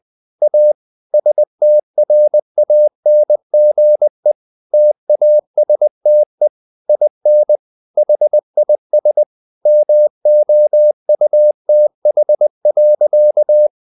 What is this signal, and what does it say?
Morse code, heard as a keyed tone, about 20 wpm, 600 Hz. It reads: A STRANGE TASTE IN HIS MOUTH.
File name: the old man and the sea part 2 20wpm